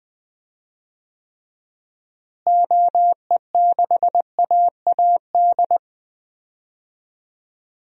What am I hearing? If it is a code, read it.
Morse code, heard as a keyed tone, about 20 wpm, 700 Hz. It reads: OE6AAD
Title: DXCC-100 Call Signs - 6 Characters - Repeated 3x - 20wpm